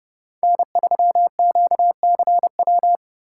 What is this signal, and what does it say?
Morse code, heard as a keyed tone, about 32 wpm, 700 Hz. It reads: N3QCW